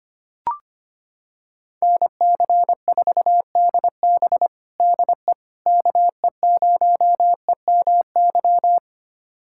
Morse code, heard as a keyed tone, about 25 wpm, 700 Hz. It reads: NC4DB DE KE0EMY
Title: USA Callsign DE Callsign 25wpm